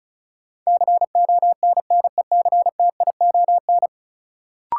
Morse code, heard as a keyed tone, about 36 wpm, 700 Hz. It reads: CONNECTION
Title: Sentences from Top 1300 Words - New Word Drill - 35wpm